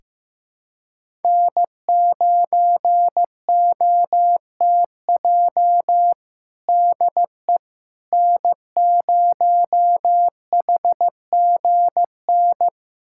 Morse code, heard as a keyed tone, about 15 wpm, 700 Hz. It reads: N9OTJ DE N0HGN
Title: USA Callsign DE Callsign - 5 Characters - 15wpm